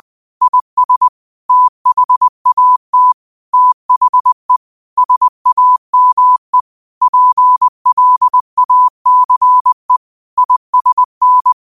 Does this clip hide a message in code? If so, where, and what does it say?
Yes: Morse code from 0.4 to 11.5 s, heard as a keyed tone, about 20 wpm, 1000 Hz. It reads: IS THAT THE SAME PLACE ISN